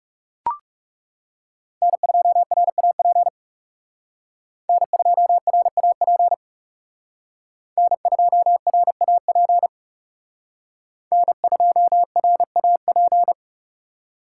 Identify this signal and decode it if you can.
Morse code, heard as a keyed tone, about 38 wpm, 700 Hz. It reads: N2RAP N2RAP N2RAP N2RAP